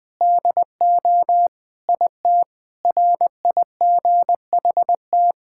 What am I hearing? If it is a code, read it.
Morse code, heard as a keyed tone, about 21 wpm, 700 Hz. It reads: DO IT RIGHT
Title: Sentences with 5-Word Phrases - Spoken Beginning - 20wpm